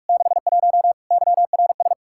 Morse code, heard as a keyed tone, about 47 wpm, 700 Hz. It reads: 61 YRS
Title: Practice QSO - Exchange 3 - More Info - 45wpm